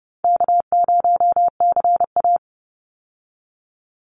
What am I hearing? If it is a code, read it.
Morse code, heard as a keyed tone, about 30 wpm, 700 Hz. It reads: K0CA
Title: Call Signs Repeated 2x - 30wpm